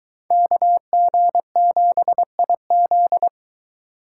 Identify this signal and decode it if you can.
Morse code, heard as a keyed tone, about 23 wpm, 700 Hz. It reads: KG7IZ